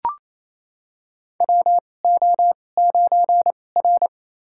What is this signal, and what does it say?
Morse code, heard as a keyed tone, about 27 wpm, spaced as for 20 wpm, 700 Hz. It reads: WO9R
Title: USA Call Signs - 4 Characters - 2x Character Spacing - 28wpm